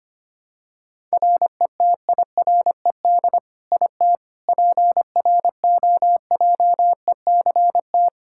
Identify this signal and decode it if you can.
Morse code, heard as a keyed tone, about 25 wpm, 700 Hz. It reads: RETIRED IT PROJECT
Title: Practice QSO - Exchange 3 - More Info - 25wpm